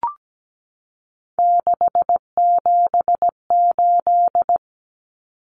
Morse code, heard as a keyed tone, about 17 wpm, 700 Hz. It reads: 678
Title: Numbers - 3-Digit 17wpm